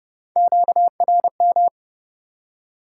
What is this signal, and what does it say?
Morse code, heard as a keyed tone, about 30 wpm, 700 Hz. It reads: QRM
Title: Top 90 QSO Elements 30wpm